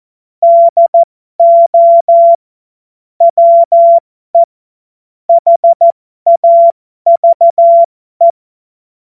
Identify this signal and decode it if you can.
Morse code, heard as a keyed tone, about 14 wpm, spaced as for 12 wpm, 680 Hz. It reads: DO WE HAVE
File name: Morsekurs / Witze